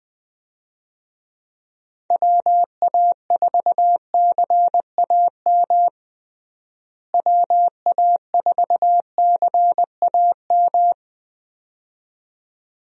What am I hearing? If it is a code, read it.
Morse code, heard as a keyed tone, about 19 wpm, 700 Hz. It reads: WA4CAM WA4CAM
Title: USA Call Signs - 6 Characters - Repeated 2x - 20wpm